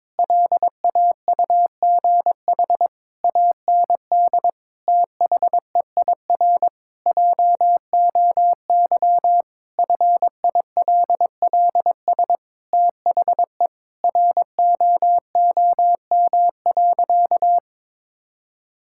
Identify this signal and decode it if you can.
Morse code, heard as a keyed tone, about 22 wpm, 700 Hz. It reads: LAUGH AND THEIR JOY FILLS THE ROOM.
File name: practice / 22wpm / Sentences with 2-Word Phrases - 22wpm